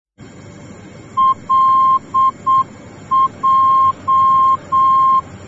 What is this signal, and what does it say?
Morse code, heard as a keyed tone, about 7 wpm, 1060 Hz. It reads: LJ
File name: germany / mx / LJ-D